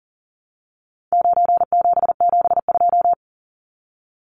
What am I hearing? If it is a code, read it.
Morse code, heard as a keyed tone, about 40 wpm, 700 Hz. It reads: 9772
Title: Numbers - 4-Digit 40wpm